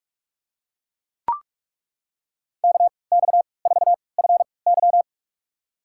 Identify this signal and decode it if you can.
Morse code, heard as a keyed tone, about 43 wpm, spaced as for 25 wpm, 700 Hz. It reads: KX4FY
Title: USA Call Signs - 5 Characters - 3x Character Spacing - 45wpm